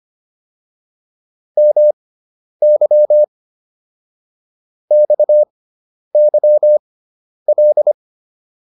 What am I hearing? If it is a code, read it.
Morse code, heard as a keyed tone, about 25 wpm, spaced as for 10 wpm, 600 Hz. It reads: MY XYL